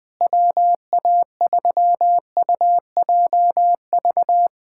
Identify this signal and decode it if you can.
Morse code, heard as a keyed tone, about 20 wpm, 700 Hz. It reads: WA3UJV
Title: USA Callsign DE Callsign 20wpm